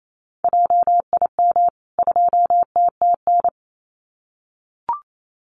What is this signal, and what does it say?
Morse code, heard as a keyed tone, about 28 wpm, 700 Hz. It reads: JIM 2TTN
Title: CWT Exchange 28wpm